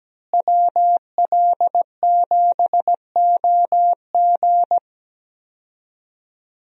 Morse code, heard as a keyed tone, about 17 wpm, 700 Hz. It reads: WL7OG
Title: USA Call Signs - 5 Characters - Repeated 2x - 17wpm